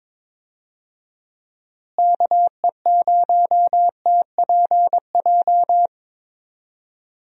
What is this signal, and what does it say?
Morse code, heard as a keyed tone, about 22 wpm, 700 Hz. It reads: KE0TPJ
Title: USA Call Signs - 6 Characters - Repeated 2x - 22wpm